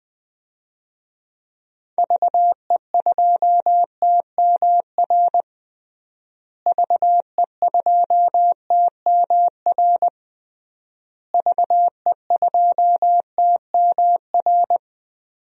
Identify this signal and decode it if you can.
Morse code, heard as a keyed tone, about 20 wpm, 700 Hz. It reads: VE2TMR VE2TMR VE2TMR
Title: DXCC-100 Call Signs - 6 Characters - Repeated 3x - 20wpm